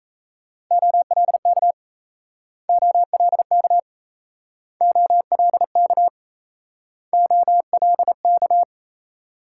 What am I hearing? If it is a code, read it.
Morse code, heard as a keyed tone, about 38 wpm, 700 Hz. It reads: OLK OLK OLK OLK